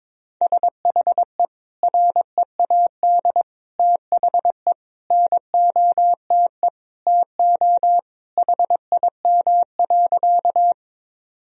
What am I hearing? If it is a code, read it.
Morse code, heard as a keyed tone, about 22 wpm, 700 Hz. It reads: SHE READ THE NOTE TO HIM.
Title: Sentences from Top 1000 Words - Review Part 2 - 22wpm